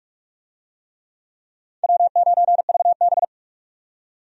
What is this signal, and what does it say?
Morse code, heard as a keyed tone, about 46 wpm, 700 Hz. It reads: W9VB
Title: USA Call Signs - 4 Characters - 45wpm